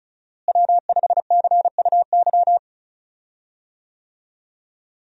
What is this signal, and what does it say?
Morse code, heard as a keyed tone, about 36 wpm, 700 Hz. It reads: W5CUY
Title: USA Call Signs - 5 Characters - Repeated 2x - 35wpm